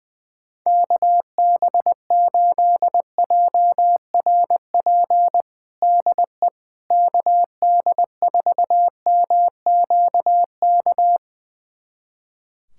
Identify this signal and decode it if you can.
Morse code, heard as a keyed tone, about 20 wpm, 700 Hz. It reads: KB8JRP DE KD4MQK
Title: USA Callsign DE Callsign - 6 Characters - 20wpm